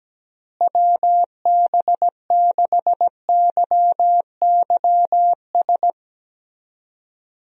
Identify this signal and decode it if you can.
Morse code, heard as a keyed tone, about 17 wpm, 700 Hz. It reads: WB6YYS
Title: USA Call Signs - 6 Characters - Repeated 2x - 17wpm